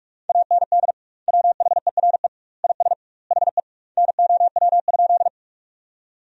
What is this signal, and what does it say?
Morse code, heard as a keyed tone, about 46 wpm, 700 Hz. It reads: AND WHERE IS HE NOW?
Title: 2x Word-Spacing - Sentences from Top 200 Words - 45wpm